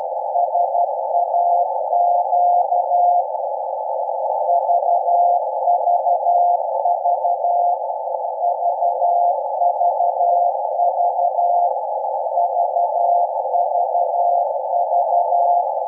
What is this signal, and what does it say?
Morse code, heard as a keyed tone, about 12 wpm, 700 Hz, with noise. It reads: SAQ VVV VVV VVV
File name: rec / 24-Dec-2012-0856 0.020MHz